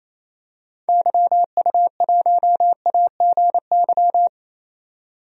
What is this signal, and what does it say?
Morse code, heard as a keyed tone, about 28 wpm, 700 Hz. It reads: YU1AGY
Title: DXCC-100 Call Signs - 6 Characters - 28wpm